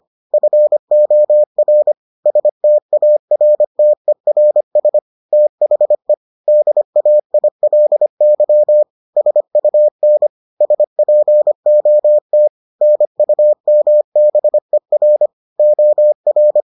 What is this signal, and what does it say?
Morse code, heard as a keyed tone, about 25 wpm, 600 Hz. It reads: FOR STARTERS THE DAILY SUN SPOT NUMBER OR